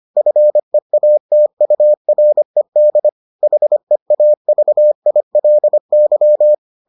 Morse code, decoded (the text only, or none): FEATURED HEAVILY